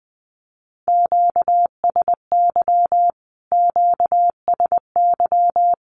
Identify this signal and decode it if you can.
Morse code, heard as a keyed tone, about 20 wpm, 700 Hz. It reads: QSY QSY
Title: Q-Codes Repeated 3x 20wpm